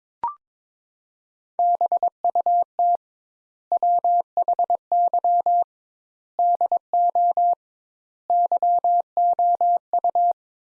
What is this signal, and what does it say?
Morse code, heard as a keyed tone, about 22 wpm, 700 Hz. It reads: BUT WHY DO YOU
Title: Sentences Top 200 Words - 22wpm